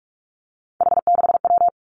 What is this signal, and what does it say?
Morse code, heard as a keyed tone, about 46 wpm, 700 Hz. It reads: H6W
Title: Three Challenging Letter-Number Combinations 45wpm